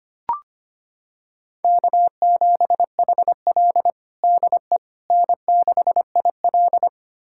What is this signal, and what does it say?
Morse code, heard as a keyed tone, about 25 wpm, 700 Hz. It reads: K7HL DE N6IL
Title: USA Callsign DE Callsign - 4 Characters - 25wpm